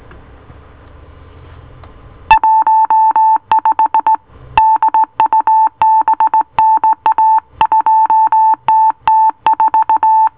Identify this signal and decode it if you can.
Morse code, heard as a keyed tone, about 19 wpm, 880 Hz. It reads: 15 DUBNA 2TT4